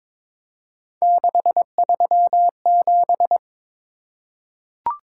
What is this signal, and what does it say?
Morse code, heard as a keyed tone, about 22 wpm, 700 Hz. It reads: 637
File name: practice / 22wpm / Three Letter-Number Combinations - 22wpm